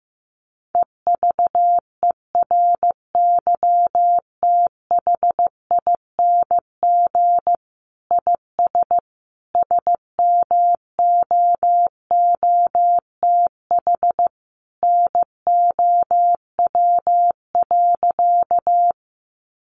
Morse code, heard as a keyed tone, about 15 wpm, 700 Hz. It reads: EVERYTHING IS SMOOTH NOW.
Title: Sentences from Top 1400 Words 15wpm